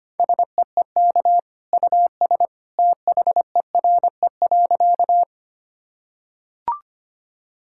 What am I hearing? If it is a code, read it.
Morse code, heard as a keyed tone, about 25 wpm, 700 Hz. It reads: SEEK US THERE.
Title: Sentences from Top 1600 Words 25wpm